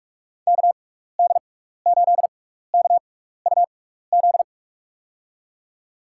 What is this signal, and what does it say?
Morse code, heard as a keyed tone, about 46 wpm, spaced as for 15 wpm, 700 Hz. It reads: KD8KUZ
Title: USA Call Signs - 6 Characters - 6x Character Spacing - 45wpm